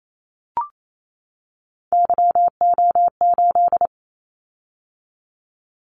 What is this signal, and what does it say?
Morse code, heard as a keyed tone, about 28 wpm, 700 Hz. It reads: YO8
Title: Callsign Prefixes for Top 25 DXCC - 28wpm